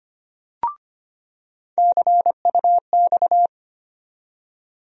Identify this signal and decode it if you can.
Morse code, heard as a keyed tone, about 25 wpm, 700 Hz. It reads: CUX
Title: USA Call Sign Postfixes - 3 Letters 25wpm